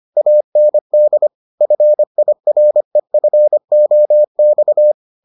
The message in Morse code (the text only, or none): AND FIREFOX